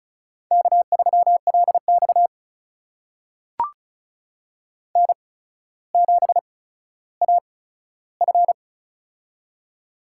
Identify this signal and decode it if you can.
Morse code, heard as a keyed tone, about 36 wpm, spaced as for 15 wpm, 700 Hz. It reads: K3LX N7AF
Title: USA Call Signs - 4 Characters - 8x Character Spacing - 35wpm